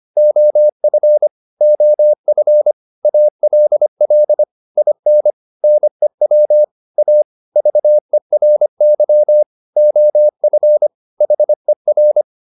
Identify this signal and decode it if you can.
Morse code, heard as a keyed tone, about 25 wpm, 600 Hz. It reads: OF OF ALL IN NEW A VERY OF HER